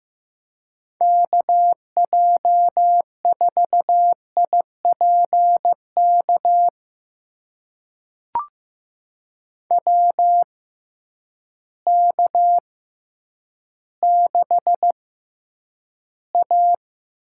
Morse code, heard as a keyed tone, about 15 wpm, spaced as for 9 wpm, 700 Hz. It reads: KJ4IPK WK6A
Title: USA Call Signs - 6x Character Spacing - 15wpm